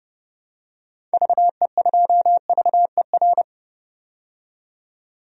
Morse code, heard as a keyed tone, about 30 wpm, 700 Hz. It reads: VE2VER
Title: DXCC-100 Call Signs - 6 Characters - 30wpm